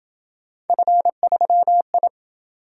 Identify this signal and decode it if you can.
Morse code, heard as a keyed tone, about 27 wpm, 700 Hz. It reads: F3I